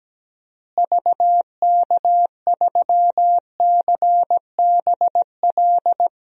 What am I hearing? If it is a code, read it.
Morse code, heard as a keyed tone, about 17 wpm, 700 Hz. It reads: VK3CBL
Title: DXCC-100 Call Signs - 6 Characters - Repeated 2x - 17wpm